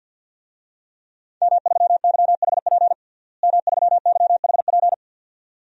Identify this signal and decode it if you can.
Morse code, heard as a keyed tone, about 51 wpm, 700 Hz. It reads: M3YHP M3YHP
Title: DXCC-100 Call Signs - 5 Characters - Repeated 2x - 50wpm